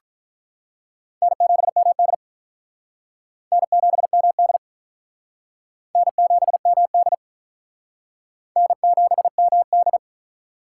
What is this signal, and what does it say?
Morse code, heard as a keyed tone, about 45 wpm, 700 Hz. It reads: N7MD N7MD N7MD N7MD